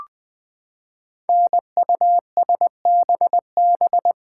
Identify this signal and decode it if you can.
Morse code, heard as a keyed tone, about 20 wpm, 700 Hz. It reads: NUSBB
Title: Code Groups - 5 Letters - 20wpm